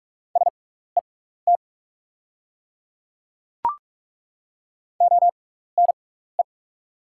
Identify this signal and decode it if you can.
Morse code, heard as a keyed tone, about 43 wpm, spaced as for 15 wpm, 690 Hz. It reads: SET ONE
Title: 6x Character Spacing - Top 300 Words - 45wpm